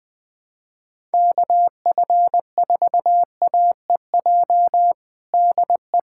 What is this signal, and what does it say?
Morse code, heard as a keyed tone, about 21 wpm, 700 Hz. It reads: KF4AEJ DE
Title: USA Callsign DE Callsign - 6 Characters - 20wpm